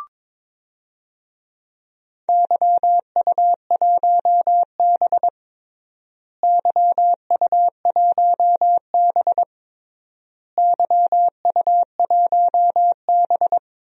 Morse code, heard as a keyed tone, about 22 wpm, 700 Hz. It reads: YU1B YU1B YU1B
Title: DXCC-100 Call Signs - 4 Characters - Repeated 3x - 22wpm